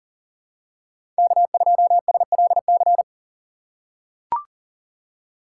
Morse code, heard as a keyed tone, about 40 wpm, 700 Hz. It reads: K2SLC
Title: USA Call Signs - 5 Characters - 6x Character Spacing - 40wpm